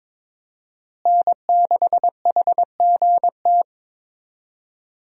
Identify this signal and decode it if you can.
Morse code, heard as a keyed tone, about 22 wpm, 700 Hz. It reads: N6HGT
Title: USA Call Signs - 5 Characters - Repeated 3x - 22wpm